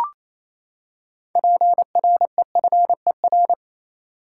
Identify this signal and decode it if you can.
Morse code, heard as a keyed tone, about 28 wpm, 700 Hz. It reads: PREFER
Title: Six Letter Words - 28wpm